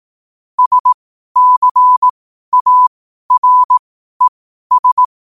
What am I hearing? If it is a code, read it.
Morse code, heard as a keyed tone, about 18 wpm, spaced as for 12 wpm, 1000 Hz. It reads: SCARES
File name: 2026-01-07-12wpm